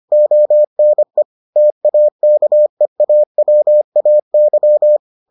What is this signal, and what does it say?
Morse code, heard as a keyed tone, about 25 wpm, 600 Hz. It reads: ONE TAKEAWAY